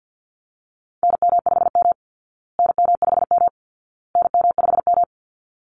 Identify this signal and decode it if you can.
Morse code, heard as a keyed tone, about 50 wpm, 700 Hz. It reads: NM5M NM5M NM5M
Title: Call Signs Repeated 2x - 50wpm